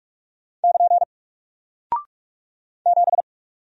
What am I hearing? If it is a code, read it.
Morse code, heard as a keyed tone, about 44 wpm, 700 Hz. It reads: <KN> 7
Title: Single Character (Letter:Number:Punctuation:Prosign) 45wpm